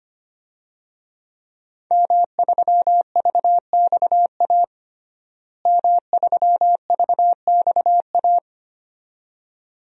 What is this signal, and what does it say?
Morse code, heard as a keyed tone, about 25 wpm, 700 Hz. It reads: M3VXA M3VXA
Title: DXCC-100 Call Signs - 5 Characters - Repeated 2x - 25wpm